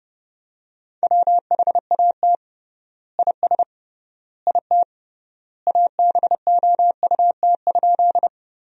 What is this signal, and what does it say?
Morse code, heard as a keyed tone, about 30 wpm, 700 Hz. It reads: WHAT IS IT ABOUT?
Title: 3x Word-Spacing - Sentences from Top 100 Words - 30wpm